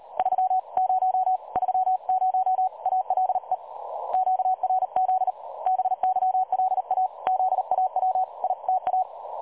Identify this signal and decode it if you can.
Morse code, heard as a keyed tone, about 36 wpm, 750 Hz, with radio noise. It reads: 30 30 ALE QRZ BYLA ZAWITA
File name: dlazaawansowanych / nagraniazpasma / websdr recording 3565.0kHz